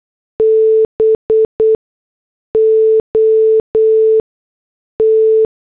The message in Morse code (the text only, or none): BOT